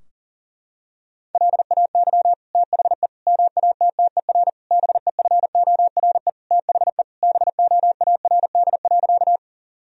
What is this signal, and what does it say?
Morse code, heard as a keyed tone, about 40 wpm, 700 Hz. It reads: LAY THE MATTER BEFORE THE BOARD.
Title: Sentences from Top 1000 Words - Review Part 4 - 40wpm